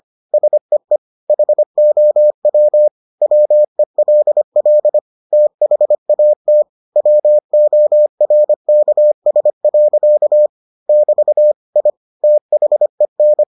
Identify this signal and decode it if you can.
Morse code, heard as a keyed tone, about 25 wpm, 600 Hz. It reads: SEE HOW WELL THAT WORKS. <BT> I THEN